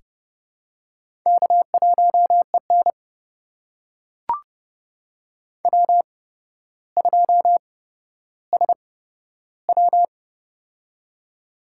Morse code, heard as a keyed tone, about 30 wpm, spaced as for 13 wpm, 700 Hz. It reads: K1EN W2SW
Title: USA Call Signs - 4 Characters - 8x Character Spacing - 30wpm